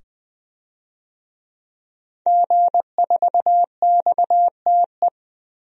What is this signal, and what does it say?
Morse code, heard as a keyed tone, about 20 wpm, 700 Hz. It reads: G4XTE
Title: DXCC-100 Call Signs - 5 Characters - 20wpm